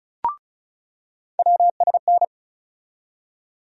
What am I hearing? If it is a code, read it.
Morse code, heard as a keyed tone, about 36 wpm, 700 Hz. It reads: WSN
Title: USA Call Sign Postfixes - 3 Letters 35wpm